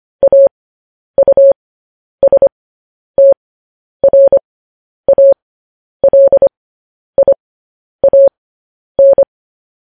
Morse code, heard as a keyed tone, about 25 wpm, spaced as for 10 wpm, 550 Hz. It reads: AUSTRALIAN
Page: news headlines in morse code at 10 wpm